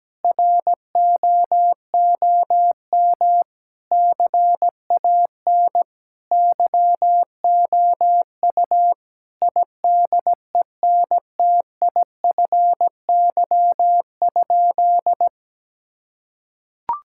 Morse code, heard as a keyed tone, about 17 wpm, 700 Hz. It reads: ROOM CAN YOU IDENTIFY?
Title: Sentences from Top 1000 Words - Review Part 6 - 17wpm